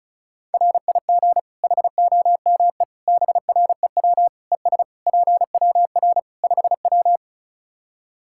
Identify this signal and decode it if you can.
Morse code, heard as a keyed tone, about 36 wpm, 700 Hz. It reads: RIG HOME BREW ES PWR 5W
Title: K1USN SST QSO Practice 35wpm